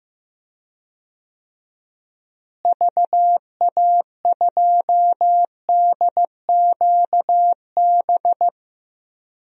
Morse code, heard as a keyed tone, about 15 wpm, 700 Hz. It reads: VA2DQB
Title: DXCC-100 Call Signs - 6 Characters - Repeated 3x - 15wpm